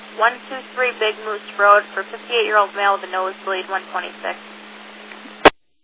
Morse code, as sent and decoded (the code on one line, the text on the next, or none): none